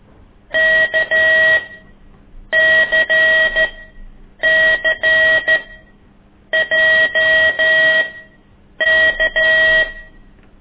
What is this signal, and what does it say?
Morse code, heard as a keyed tone, about 13 wpm, spaced as for 7 wpm, 1900 Hz, with noise. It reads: KCCJK
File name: ik1weg2 / CORSI CW